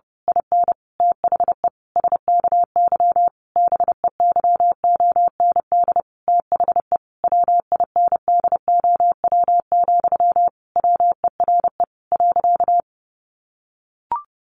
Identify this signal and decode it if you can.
Morse code, heard as a keyed tone, about 30 wpm, 700 Hz. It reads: IN THE SKY BEYOND THE WINDOW, WERE .
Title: A Sound Of Thunder 30wpm